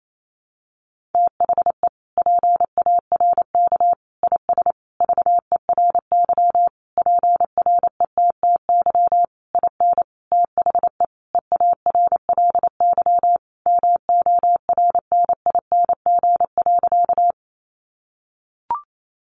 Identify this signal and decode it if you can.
Morse code, heard as a keyed tone, about 28 wpm, 700 Hz. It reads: THE PARK IS VERY PRETTY IN THE EARLY MORNING.
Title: Sentences from Top 1300 Words 28wpm